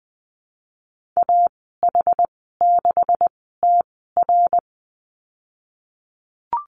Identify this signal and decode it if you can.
Morse code, heard as a keyed tone, about 20 wpm, spaced as for 15 wpm, 700 Hz. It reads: AH6TR
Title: 167 Lesson 27g - Number 7 - Call Signs 20wpm